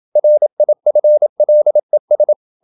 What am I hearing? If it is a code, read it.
Morse code, heard as a keyed tone, about 27 wpm, 600 Hz. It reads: RIFLES
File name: Never 0027wpm